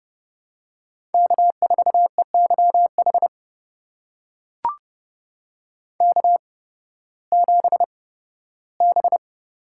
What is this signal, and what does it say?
Morse code, heard as a keyed tone, about 30 wpm, spaced as for 16 wpm, 700 Hz. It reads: K4EYH K7B
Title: USA Call Signs - 5 Characters - 8x Character Spacing - 30wpm